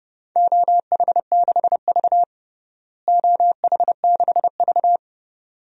Morse code, heard as a keyed tone, about 30 wpm, 700 Hz. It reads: OH6V OH6V
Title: DXCC-100 Call Signs - 4 Characters - Repeated 2x - 30wpm